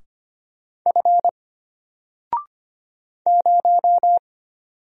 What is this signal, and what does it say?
Morse code, heard as a keyed tone, about 25 wpm, 700 Hz. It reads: F 0
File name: Single-Letter-Number-25wpm